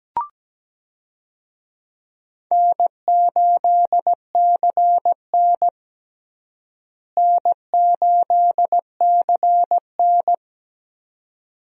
Morse code, heard as a keyed tone, about 17 wpm, 700 Hz. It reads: N8CN N8CN
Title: USA Call Signs - 4 Characters - Repeated 2x - 17wpm